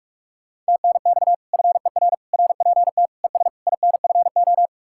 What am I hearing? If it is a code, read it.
Morse code, heard as a keyed tone, about 49 wpm, 700 Hz. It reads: TNX FER RPT ES INFO